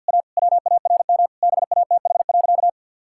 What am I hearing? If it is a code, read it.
Morse code, heard as a keyed tone, about 50 wpm, 700 Hz. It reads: A WARM BATH.
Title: Sentences from Top 1600 Words 50wpm